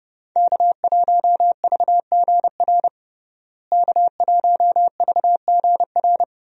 Morse code, heard as a keyed tone, about 30 wpm, 700 Hz. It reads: K1VGR K1VGR
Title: USA Call Signs - 5 Characters - Repeated 2x - 30wpm